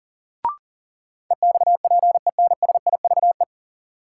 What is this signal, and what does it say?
Morse code, heard as a keyed tone, about 40 wpm, 700 Hz. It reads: EXPENSIVE
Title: Sentences from Top 1500 Words - New Word Drill Alt - 40wpm